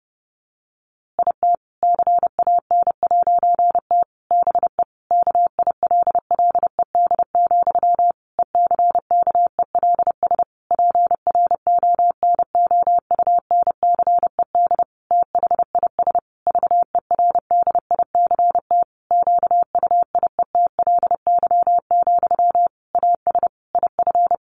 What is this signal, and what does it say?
Morse code, heard as a keyed tone, about 30 wpm, 700 Hz. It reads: IT CAN'T BE KILLED, ECKELS PRONOUNCED THIS VERDICT QUIETLY, AS IF
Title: A Sound Of Thunder 30wpm